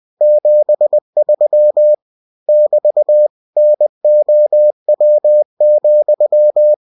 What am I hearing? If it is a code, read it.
Morse code, heard as a keyed tone, about 20 wpm, 600 Hz. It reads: 73 <BT> NOW,
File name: cw-books / winnie-the-pooh / winnie-20wpm-page-0073